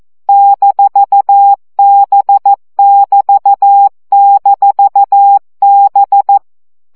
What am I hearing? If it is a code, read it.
Morse code, heard as a keyed tone, about 14 wpm, 800 Hz. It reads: -B<BT>-B
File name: CW gyakorlas b 6 egyenlo minusz 12 baud